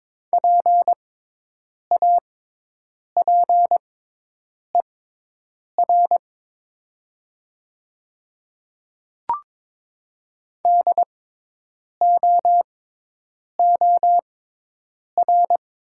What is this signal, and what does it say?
Morse code, heard as a keyed tone, about 22 wpm, spaced as for 8 wpm, 700 Hz. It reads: PAPER DOOR
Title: 6x Character Spacing - Top 300 Words - 22wpm